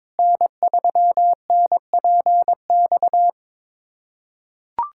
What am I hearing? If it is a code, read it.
Morse code, heard as a keyed tone, about 22 wpm, 700 Hz. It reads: N3NPX
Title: USA Callsign DE Callsign - 5 Characters - 22wpm